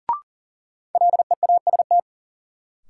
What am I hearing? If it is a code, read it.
Morse code, heard as a keyed tone, about 40 wpm, 700 Hz. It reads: LEAST
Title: Sentences from Top 700 Words - New Word Drill - 40wpm